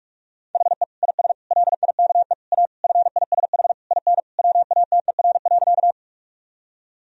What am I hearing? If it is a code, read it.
Morse code, heard as a keyed tone, about 41 wpm, 700 Hz. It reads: HE IS LIKE A FISH IN WATER.
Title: Sentences from Top 1000 Words - Review Part 1 - 45wpm